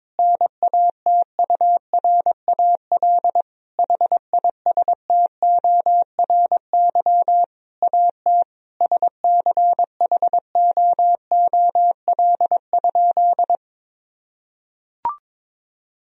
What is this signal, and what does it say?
Morse code, heard as a keyed tone, about 22 wpm, 700 Hz. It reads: NATURAL HISTORY AT SCHOOL?
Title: Sentences from Top 1000 Words - Review Part 4 - 22wpm